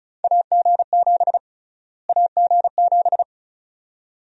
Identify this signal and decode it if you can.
Morse code, heard as a keyed tone, about 34 wpm, 700 Hz. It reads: AG7 AG7
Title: USA Call Sign Prefixes - Repeated 2x - 35wpm